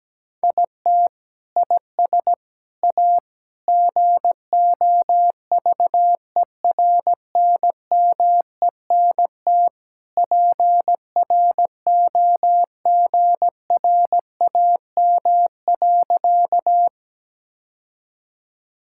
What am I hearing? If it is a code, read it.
Morse code, heard as a keyed tone, about 17 wpm, 700 Hz. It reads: IT IS A GOVERNMENT PROGRAM.
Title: Sentences from Top 1000 Words - Review Part 5 - 17wpm